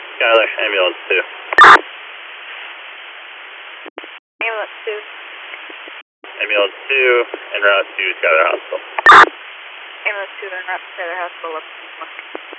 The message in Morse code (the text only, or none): none